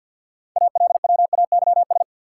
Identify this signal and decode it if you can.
Morse code, heard as a keyed tone, about 52 wpm, 700 Hz. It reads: ALWAYS
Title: Sentences from Top 300 Words - New Word Drill - 50wpm